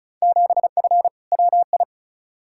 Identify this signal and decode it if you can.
Morse code, heard as a keyed tone, about 36 wpm, 700 Hz. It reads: 7F WI
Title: Field Day Exchanges Encore - 35wpm